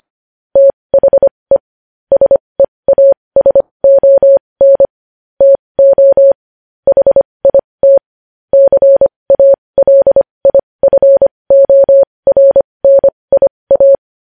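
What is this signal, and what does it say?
Morse code, heard as a keyed tone, about 25 wpm, spaced as for 20 wpm, 550 Hz. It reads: THE SEASON TO HIT CALIFORNIA
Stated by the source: Morse news headlines sent at 20, opening with vvv.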